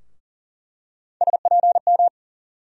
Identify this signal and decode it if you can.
Morse code, heard as a keyed tone, about 40 wpm, 700 Hz. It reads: SPM
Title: USA Call Sign Postfixes - 3 Letters 40wpm